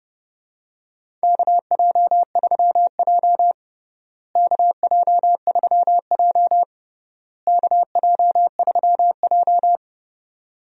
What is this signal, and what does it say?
Morse code, heard as a keyed tone, about 30 wpm, 700 Hz. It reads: KJ3J KJ3J KJ3J
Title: USA Call Signs - 4 Characters - Repeated 3x - 30wpm